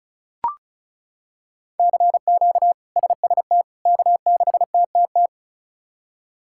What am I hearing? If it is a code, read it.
Morse code, heard as a keyed tone, about 36 wpm, 700 Hz. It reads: CQ SST K6TTT
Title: K1USN SST QSO Practice - 35wpm